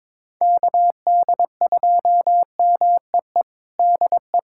Morse code, heard as a keyed tone, about 22 wpm, 700 Hz. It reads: KD2MEE DE
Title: USA Callsign DE Callsign - 6 Characters - 22wpm